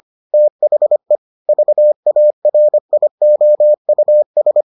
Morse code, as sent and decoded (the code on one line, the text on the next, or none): - .... / ...- .- .-. .. --- ..- ...
THE VARIOUS